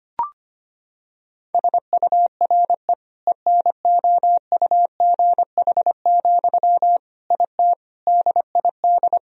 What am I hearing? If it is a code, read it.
Morse code, heard as a keyed tone, about 25 wpm, 700 Hz. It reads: SURE ENOUGH, IT DID
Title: Sentences from Top 1000 Words - Review Part 2 - 25wpm